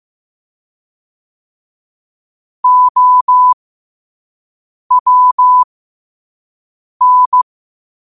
Morse code, heard as a keyed tone, about 15 wpm, spaced as for 5 wpm, 1000 Hz. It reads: OWN
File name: qotdfiles / 5wpm